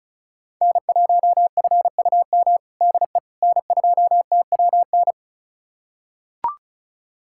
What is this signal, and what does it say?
Morse code, heard as a keyed tone, about 36 wpm, 700 Hz. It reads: N1FUM DE N2TWN
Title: USA Callsign DE Callsign - 5 Characters - 35wpm